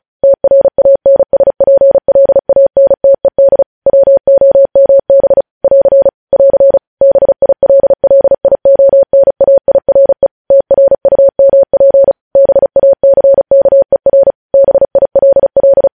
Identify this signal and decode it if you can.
Morse code, heard as a keyed tone, about 36 wpm, 550 Hz. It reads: TRANSPLANTED WOMB <AR> <AR> BILLIONAIRE TRUMP BACKER BILL
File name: podcast / News headlines-1x-35x35